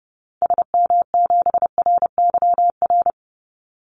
Morse code, heard as a keyed tone, about 30 wpm, 700 Hz. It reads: SM7RYR
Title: Call Signs Repeated 2x - 30wpm